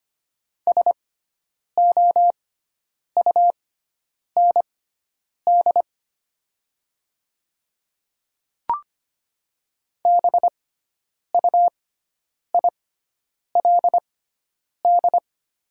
Morse code, heard as a keyed tone, about 25 wpm, spaced as for 9 wpm, 700 Hz. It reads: SOUND BUILD
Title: 6x Character Spacing - Top 300 Words - 25wpm